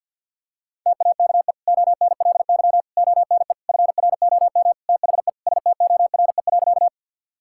Morse code, heard as a keyed tone, about 50 wpm, 700 Hz. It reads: TAKE ONLY ONE FROM THE STORE.
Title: Sentences from Top 1000 Words - Review Part 5 - 50wpm